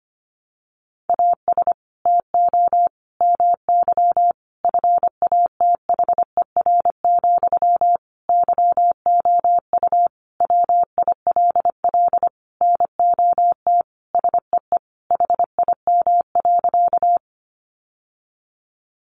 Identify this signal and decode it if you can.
Morse code, heard as a keyed tone, about 25 wpm, 700 Hz. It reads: AS TO MY FATHER, YOU WILL NOT SEE HIM.
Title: Sentences from Top 600 Words 25wpm